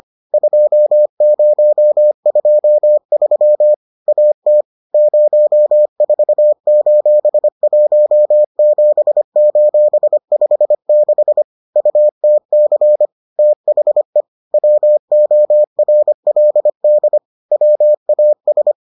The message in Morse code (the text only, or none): 2023 AT 04:17:56 UTC THE WORLD WAS